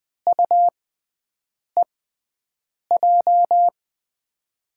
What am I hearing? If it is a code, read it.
Morse code, heard as a keyed tone, about 21 wpm, spaced as for 7 wpm, 700 Hz. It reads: UEJ
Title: USA Call Signs - 6 Characters - 6x Character Spacing - 20wpm